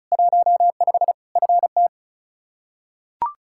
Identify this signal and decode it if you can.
Morse code, heard as a keyed tone, about 36 wpm, 700 Hz. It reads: 15 FT
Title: Practice QSO - Exchange 2 - Info - 35wpm